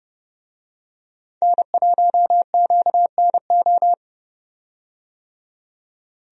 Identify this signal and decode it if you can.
Morse code, heard as a keyed tone, about 30 wpm, 700 Hz. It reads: N1QNO
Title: USA Call Signs - 5 Characters - 30wpm